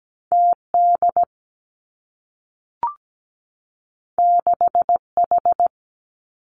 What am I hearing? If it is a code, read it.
Morse code, heard as a keyed tone, about 17 wpm, 700 Hz. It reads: TD 6H
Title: Two Challenging Letter-Number Combinations 17wpm